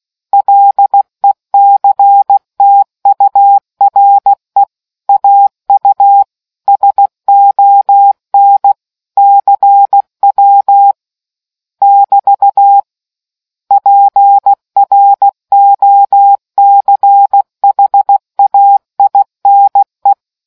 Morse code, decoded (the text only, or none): LECTURE AU SON CW <BT> PROCHAINE